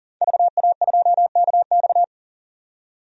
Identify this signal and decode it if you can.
Morse code, heard as a keyed tone, about 39 wpm, 700 Hz. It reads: VA2KX
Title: DXCC-100 Call Signs - 5 Characters - 40wpm